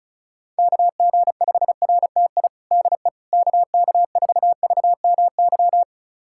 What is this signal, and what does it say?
Morse code, heard as a keyed tone, about 36 wpm, 700 Hz. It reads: KG5RTI DE KK4VMY
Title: USA Callsign DE Callsign 35wpm